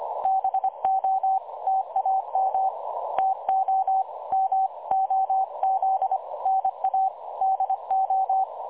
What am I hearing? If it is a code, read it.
Morse code, heard as a keyed tone, about 23 wpm, 770 Hz. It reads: BO TAM TO MOZNA DO
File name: dlazaawansowanych / nagraniazpasma / websdr recording 3565.0kHz